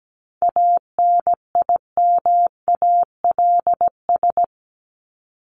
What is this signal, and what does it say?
Morse code, heard as a keyed tone, about 17 wpm, 700 Hz. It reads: ANIMALS
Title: ES:S Suffix - Top 100 Words with Base Words 17wpm